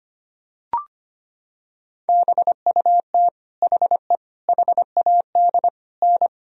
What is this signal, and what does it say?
Morse code, heard as a keyed tone, about 25 wpm, 700 Hz. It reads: BUT HE HAD N
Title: Sentences from Top 100 Words and Abbreviations 25wpm